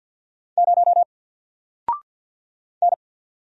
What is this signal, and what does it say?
Morse code, heard as a keyed tone, about 48 wpm, 700 Hz. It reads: 0 N
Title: Single Letter-Number Alt 50wpm